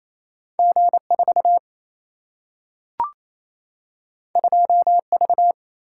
Morse code, heard as a keyed tone, about 28 wpm, 700 Hz. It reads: G 4 2 V
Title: Single-Letter Single-Number 28wpm